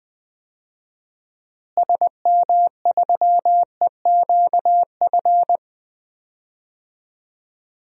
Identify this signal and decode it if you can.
Morse code, heard as a keyed tone, about 20 wpm, 700 Hz. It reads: SM3EQF
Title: DXCC-100 Call Signs - 6 Characters - Repeated 3x - 20wpm